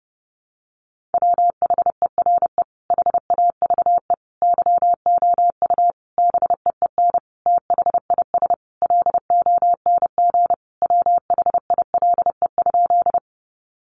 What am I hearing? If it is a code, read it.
Morse code, heard as a keyed tone, about 30 wpm, 700 Hz. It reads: WHERE HAVE YOU BEEN THIS LONG WHILE?
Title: Sentences from Top 300 Words 30wpm